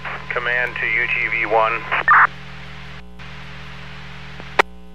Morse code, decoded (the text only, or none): none